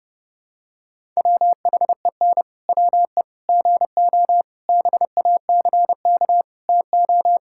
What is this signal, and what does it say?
Morse code, heard as a keyed tone, about 32 wpm, 700 Hz. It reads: WHEN WE GO BACK TO
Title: Pangram Challenge v2 - 30wpm